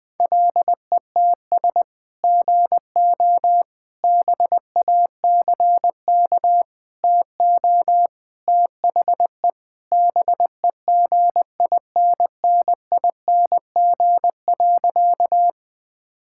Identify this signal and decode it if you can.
Morse code, heard as a keyed tone, about 21 wpm, 700 Hz. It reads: LETS GO BACK TO THE BEGINNING.
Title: Sentences with 3-Word Phrases - 20wpm